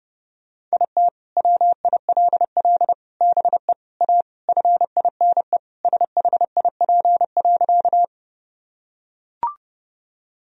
Morse code, heard as a keyed tone, about 29 wpm, 700 Hz. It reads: IT WILL BE A FINE SHIP.
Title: Sentences from Top 1000 Words - Review Part 2 - 30wpm